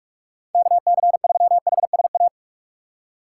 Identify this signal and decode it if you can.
Morse code, heard as a keyed tone, about 46 wpm, 700 Hz. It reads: KC3HSA
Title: USA Call Signs - 6 Characters - 3x Character Spacing - 45wpm